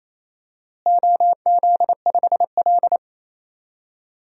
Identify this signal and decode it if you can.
Morse code, heard as a keyed tone, about 28 wpm, 700 Hz. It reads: OZ5L
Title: DXCC-100 Call Signs - 4 Characters - Repeated 3x - 28wpm